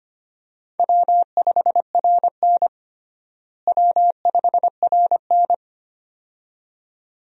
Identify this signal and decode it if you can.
Morse code, heard as a keyed tone, about 25 wpm, 700 Hz. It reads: W5RN W5RN
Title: USA Call Signs - 4 Characters - Repeated 2x - 25wpm